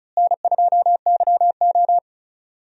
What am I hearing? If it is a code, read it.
Morse code, heard as a keyed tone, about 36 wpm, 700 Hz. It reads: N2YO
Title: USA Call Signs - 4 Characters - Repeated 3x - 35wpm